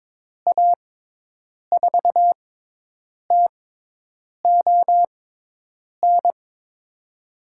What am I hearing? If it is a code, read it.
Morse code, heard as a keyed tone, about 22 wpm, spaced as for 8 wpm, 700 Hz. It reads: A4TON
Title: USA Call Signs - 6 Characters - 6x Character Spacing - 22wpm